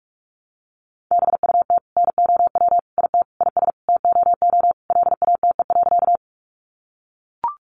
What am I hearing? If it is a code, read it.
Morse code, heard as a keyed tone, about 44 wpm, 700 Hz. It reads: BUT NOW IT IS TOO LATE.
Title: Sentences from Top 300 Words 45wpm